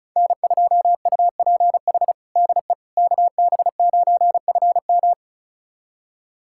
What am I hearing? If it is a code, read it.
Morse code, heard as a keyed tone, about 36 wpm, 700 Hz. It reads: N2UPH DE KB9FM
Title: USA Callsign DE Callsign - 5 Characters - 35wpm